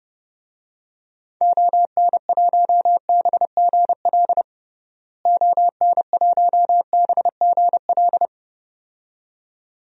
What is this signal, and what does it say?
Morse code, heard as a keyed tone, about 30 wpm, 700 Hz. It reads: ON1BGL ON1BGL
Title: DXCC-100 Call Signs - 6 Characters - Repeated 2x - 30wpm